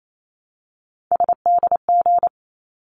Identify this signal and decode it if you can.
Morse code, heard as a keyed tone, about 28 wpm, 700 Hz. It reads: SDG
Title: ARRL Sections 28wpm